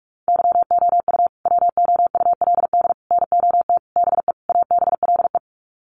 Morse code, heard as a keyed tone, about 44 wpm, 700 Hz. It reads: YOU WOULD NOT BE ABLE
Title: Sets of 5 Words Encore2 45wpm